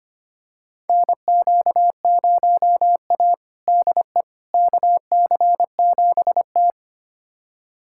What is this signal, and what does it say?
Morse code, heard as a keyed tone, about 25 wpm, 700 Hz. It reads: NQ0A DE KC7T
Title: USA Callsign DE Callsign - 4 Characters - 25wpm